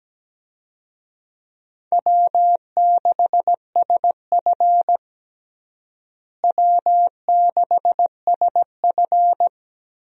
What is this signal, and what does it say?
Morse code, heard as a keyed tone, about 17 wpm, 700 Hz. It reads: W6SF W6SF
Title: USA Call Signs - 4 Characters - Repeated 3x - 17wpm